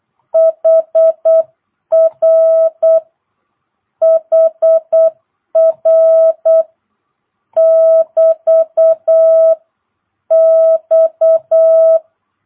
Morse code, heard as a keyed tone, about 8 wpm, 660 Hz. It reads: HR HR <BT> X
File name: jh8cbh / 2022cw